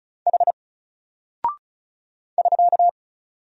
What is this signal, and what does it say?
Morse code, heard as a keyed tone, about 36 wpm, 700 Hz. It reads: H <SK>
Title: Single Character (Letter:Number:Punctuation:Prosign) Alt 35wpm